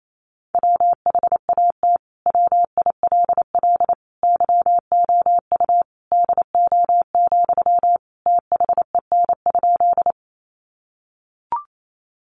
Morse code, heard as a keyed tone, about 28 wpm, 700 Hz. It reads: WHAT WILL YOU DO, THEN?
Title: Sentences from Top 100 Words 28wpm